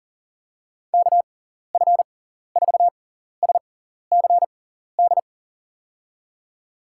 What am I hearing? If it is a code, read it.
Morse code, heard as a keyed tone, about 40 wpm, spaced as for 14 wpm, 700 Hz. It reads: KF4SCD
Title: USA Call Signs - 6 Characters - 6x Character Spacing - 40wpm